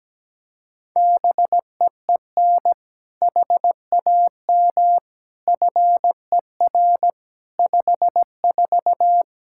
Morse code, heard as a keyed tone, about 17 wpm, 700 Hz. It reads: BEEN HAM FER 54
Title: K1USN SST QSO Practice 17wpm